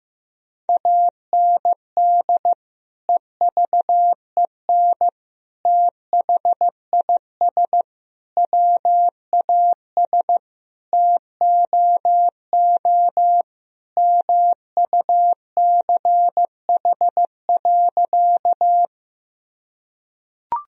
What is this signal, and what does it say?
Morse code, heard as a keyed tone, about 15 wpm, 700 Hz. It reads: AND EVEN THIS WAS TOO MUCH.
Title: Sentences from Top 1000 Words - Review Part 1 - 15wpm